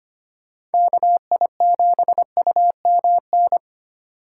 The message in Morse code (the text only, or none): KI7UMN